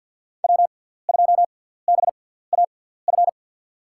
Morse code, heard as a keyed tone, about 52 wpm, spaced as for 17 wpm, 700 Hz. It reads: W2BAF
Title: USA Call Signs - 5 Characters - 6x Character Spacing - 50wpm